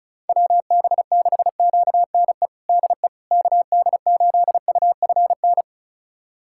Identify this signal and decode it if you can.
Morse code, heard as a keyed tone, about 36 wpm, 690 Hz. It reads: WB6QNE DE KD8UFN
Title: USA Callsign DE Callsign - 6 Characters - 35wpm